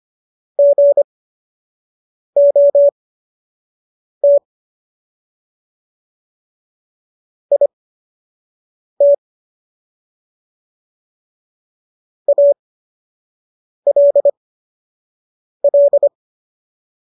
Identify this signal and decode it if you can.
Morse code, heard as a keyed tone, about 25 wpm, spaced as for 6 wpm, 570 Hz. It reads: GOT IT ALL
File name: qso002 06